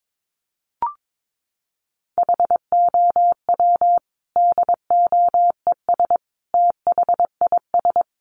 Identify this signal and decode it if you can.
Morse code, heard as a keyed tone, about 22 wpm, 700 Hz. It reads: HOW DOES THIS
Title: Sentences from Top 400 Words 22wpm